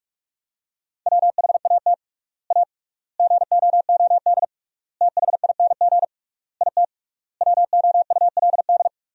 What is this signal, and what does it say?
Morse code, heard as a keyed tone, about 46 wpm, 700 Hz. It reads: WHAT A GOOD THING IT WOULD
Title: 3x Word-Spacing - Sentences from Top 200 Words - 45wpm